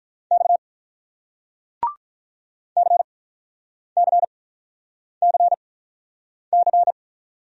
Morse code, heard as a keyed tone, about 48 wpm, spaced as for 9 wpm, 700 Hz. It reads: X CCCC